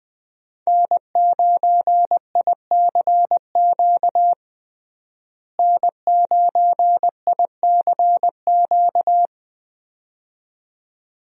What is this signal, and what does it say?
Morse code, heard as a keyed tone, about 20 wpm, 700 Hz. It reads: N9ICQ N9ICQ
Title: USA Call Signs - 5 Characters - Repeated 2x - 20wpm